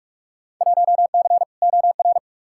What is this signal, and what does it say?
Morse code, heard as a keyed tone, about 46 wpm, 700 Hz. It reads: 1C OR
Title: Field Day Exchanges - 45wpm